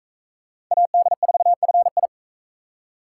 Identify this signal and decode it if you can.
Morse code, heard as a keyed tone, about 38 wpm, 700 Hz. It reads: AD4FI